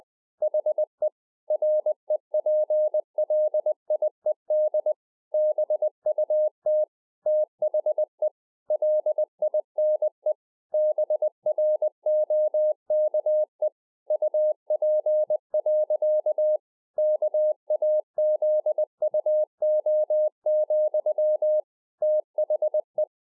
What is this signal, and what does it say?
Morse code, heard as a keyed tone, about 20 wpm, 610 Hz. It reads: HE REPLIED BUT THE LINE BROKE UP. KAZUO, THE